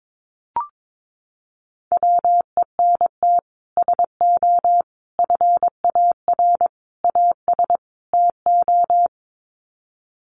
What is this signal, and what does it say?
Morse code, heard as a keyed tone, about 22 wpm, 700 Hz. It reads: WENT SO FAR AS TO
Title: Sets of 5 Words Encore 22wpm